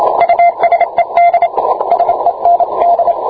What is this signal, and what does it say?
Morse code, heard as a keyed tone, about 26 wpm, 690 Hz, with radio noise. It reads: USED SEND